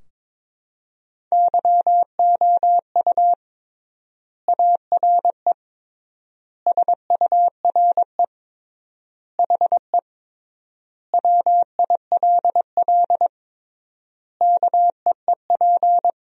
Morse code, heard as a keyed tone, about 22 wpm, 700 Hz. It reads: YOU ARE SURE HE WILL KEEP
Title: 3x Word-Spacing - Sentences from Top 300 Words - 22wpm